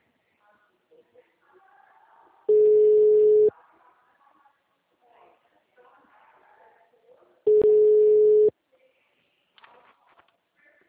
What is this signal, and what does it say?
Ringback tone (Europe, 425 Hz), with speech.